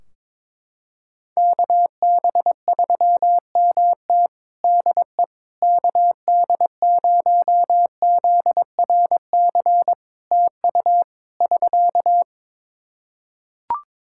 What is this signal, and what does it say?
Morse code, heard as a keyed tone, about 22 wpm, 700 Hz. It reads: KB3MT DE KD0ZRC TU <SK>
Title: Practice QSO - Exchange 4 - Ending - 22wpm